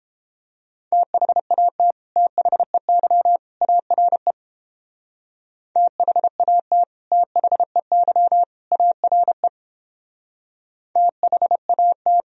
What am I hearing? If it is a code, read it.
Morse code, heard as a keyed tone, about 29 wpm, 700 Hz. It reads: THAT THEY ARE THAT THEY ARE THAT